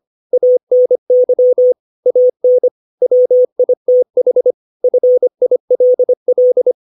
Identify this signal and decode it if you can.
Morse code, heard as a keyed tone, about 25 wpm, 500 Hz. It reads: ANY AN WITH FILL
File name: 632 words / 632 Words 25WPM